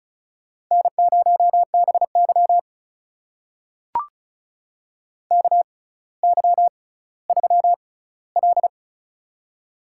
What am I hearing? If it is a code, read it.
Morse code, heard as a keyed tone, about 34 wpm, spaced as for 18 wpm, 700 Hz. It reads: N0BY KY3L
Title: USA Call Signs - 4 Characters - 6x Character Spacing - 35wpm